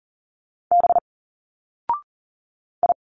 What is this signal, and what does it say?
Morse code, heard as a keyed tone, about 40 wpm, 700 Hz. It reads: B I
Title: Single Letters 40wpm